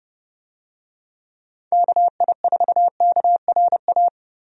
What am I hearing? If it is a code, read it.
Morse code, heard as a keyed tone, about 30 wpm, 700 Hz. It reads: KI4KRA
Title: USA Call Signs - 6 Characters - Repeated 3x - 30wpm